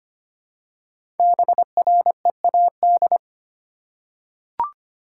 Morse code, heard as a keyed tone, about 25 wpm, 700 Hz. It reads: BREAD
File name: practice / 25wpm / Easy Words with Hints - 25wpm